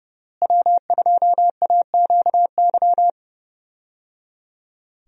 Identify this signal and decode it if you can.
Morse code, heard as a keyed tone, about 30 wpm, 700 Hz. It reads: W2AQY
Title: USA Call Signs - 5 Characters - Repeated 3x - 30wpm